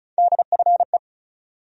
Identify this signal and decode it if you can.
Morse code, heard as a keyed tone, about 36 wpm, 700 Hz. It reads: DFE
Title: USA Call Sign Postfixes - 3 Letters 35wpm